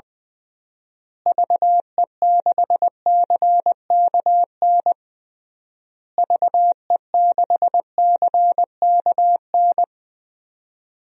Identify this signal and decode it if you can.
Morse code, heard as a keyed tone, about 21 wpm, 700 Hz. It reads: VE6CKN VE6CKN
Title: DXCC-100 Call Signs - 6 Characters - Repeated 3x - 20wpm